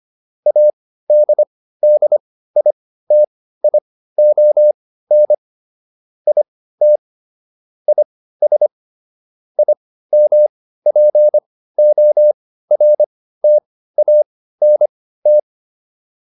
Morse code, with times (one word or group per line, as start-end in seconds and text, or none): 0.5-5.3 ADDITION
6.3-7.0 IT
7.9-8.7 IS
9.6-15.4 IMPORTANT